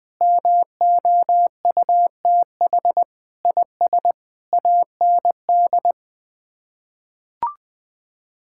Morse code, heard as a keyed tone, about 20 wpm, 700 Hz. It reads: MOUTH IS AND
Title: Sets of 8 Words - 20wpm